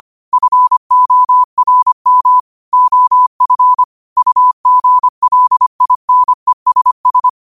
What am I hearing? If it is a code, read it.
Morse code, heard as a keyed tone, about 25 wpm, 1000 Hz. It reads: FORM OF UGLINESS
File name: qotdfiles / 25wpm